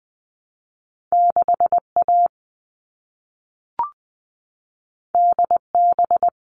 Morse code, heard as a keyed tone, about 20 wpm, 700 Hz. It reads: 6A DB
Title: All Cut Numbers - 2-Digit 20wpm